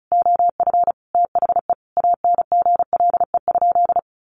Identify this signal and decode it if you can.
Morse code, heard as a keyed tone, about 34 wpm, 700 Hz. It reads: OF THE ANGLE?
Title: Sentences from Top 1500 Words 35wpm